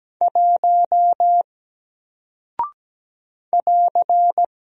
Morse code, heard as a keyed tone, about 17 wpm, 700 Hz. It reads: 1 <AR>
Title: Single Character (Letter:Number:Punctuation:Prosign) Alt 17wpm